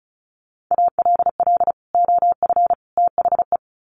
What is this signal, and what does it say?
Morse code, heard as a keyed tone, about 34 wpm, 700 Hz. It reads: ALL OF THE
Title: Sets of 3 Words 35wpm